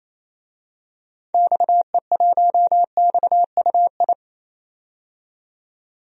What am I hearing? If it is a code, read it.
Morse code, heard as a keyed tone, about 28 wpm, 700 Hz. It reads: XE1XUI
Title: DXCC-100 Call Signs - 6 Characters - 28wpm